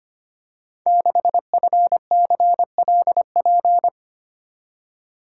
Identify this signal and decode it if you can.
Morse code, heard as a keyed tone, about 25 wpm, 700 Hz. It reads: 6FCLP
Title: Code Groups - 5 Letter:Numbers - 25wpm